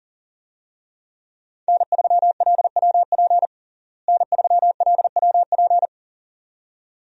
Morse code, heard as a keyed tone, about 40 wpm, 700 Hz. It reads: N3LWP N3LWP
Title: USA Call Signs - 5 Characters - Repeated 2x - 40wpm